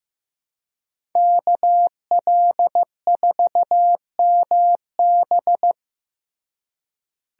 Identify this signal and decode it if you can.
Morse code, heard as a keyed tone, about 15 wpm, 700 Hz. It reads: KL4MB
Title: USA Call Signs - 5 Characters - Repeated 2x - 15wpm